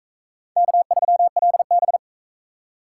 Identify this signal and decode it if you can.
Morse code, heard as a keyed tone, about 40 wpm, 700 Hz. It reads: K3LB